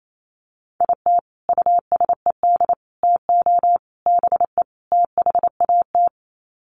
Morse code, heard as a keyed tone, about 28 wpm, 700 Hz. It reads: IT USED TO BE THAT
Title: Sets of 5 Words Encore2 28wpm